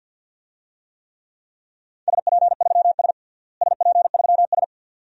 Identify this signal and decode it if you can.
Morse code, heard as a keyed tone, about 48 wpm, 700 Hz. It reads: SP3S SP3S
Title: DXCC-100 Call Signs - 4 Characters - Repeated 2x - 50wpm